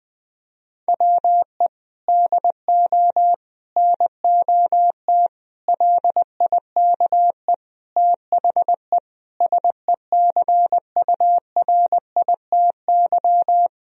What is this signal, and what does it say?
Morse code, heard as a keyed tone, about 20 wpm, 700 Hz. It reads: WE DO NOT LIKE THE SECURITY
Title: Sentences from Top 1000 Words - Review Part 4 - 20wpm